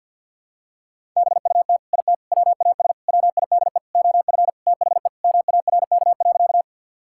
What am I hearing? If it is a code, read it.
Morse code, heard as a keyed tone, about 52 wpm, 700 Hz. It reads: BUT IT WAS WIDE OF THE MARK.
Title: Sentences from Top 1000 Words - Review Part 6 - 50wpm